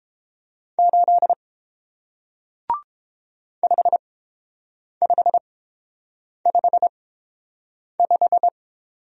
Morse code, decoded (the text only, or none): 8 5555